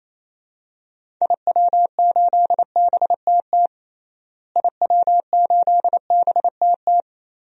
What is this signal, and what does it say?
Morse code, heard as a keyed tone, about 28 wpm, 700 Hz. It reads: IW8BTT IW8BTT
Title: DXCC-100 Call Signs - 6 Characters - Repeated 3x - 28wpm